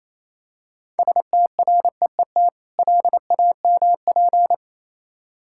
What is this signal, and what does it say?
Morse code, heard as a keyed tone, about 28 wpm, 700 Hz. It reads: STREET LAMP